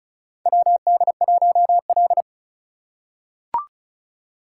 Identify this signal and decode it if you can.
Morse code, heard as a keyed tone, about 36 wpm, 700 Hz. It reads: WD1L
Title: USA Call Signs - 4 Characters - 3x Character Spacing - 35wpm